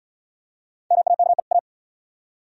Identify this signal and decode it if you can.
Morse code, heard as a keyed tone, about 73 wpm, 700 Hz. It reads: MAKE A